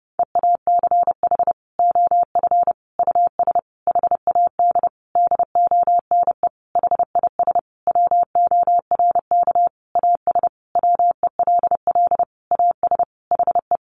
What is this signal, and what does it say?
Morse code, heard as a keyed tone, about 28 wpm, 700 Hz. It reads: EACH OF US HAD DONE HIS WORK AS WELL AS HE
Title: Sentences from Top 500 Words 30wpm